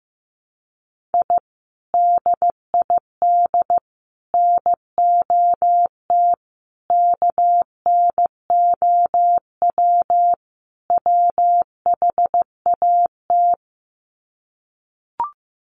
Morse code, heard as a keyed tone, about 14 wpm, 700 Hz. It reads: I DID NOT KNOW WHAT
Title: Sets of 5 Words 15wpm